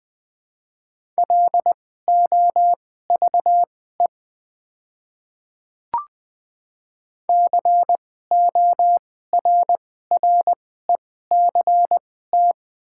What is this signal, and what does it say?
Morse code, heard as a keyed tone, about 20 wpm, spaced as for 14 wpm, 700 Hz. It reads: LOVE CORRECT
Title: 2x Character Spacing - Top 500 Words - 20wpm